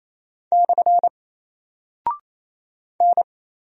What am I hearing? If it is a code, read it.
Morse code, heard as a keyed tone, about 28 wpm, 700 Hz. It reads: / N
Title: Single Character (Letter:Number:Punctuation:Prosign) 28wpm